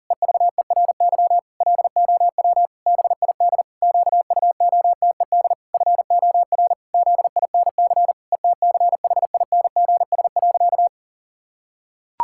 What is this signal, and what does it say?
Morse code, heard as a keyed tone, about 40 wpm, 700 Hz. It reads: EVERY LOW BID QUOTED FOR ZINC ETCHINGS.
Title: Pangram Challenge v2 - 40wpm